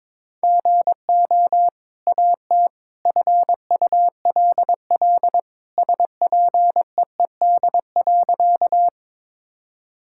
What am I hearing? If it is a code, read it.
Morse code, heard as a keyed tone, about 22 wpm, 700 Hz. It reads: GO AT FULL SPEED.
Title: Sentences from Top 1000 Words - Review Part 5 - 22wpm